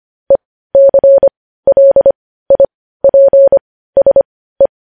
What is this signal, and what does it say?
Morse code, heard as a keyed tone, about 25 wpm, spaced as for 15 wpm, 550 Hz. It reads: ECLIPSE
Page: news headlines in morse code at 15 wpm